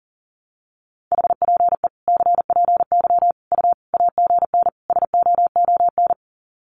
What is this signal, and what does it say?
Morse code, heard as a keyed tone, about 40 wpm, 700 Hz. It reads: HPE CPY U AGN SOON
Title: Common QSO Phrases 40wpm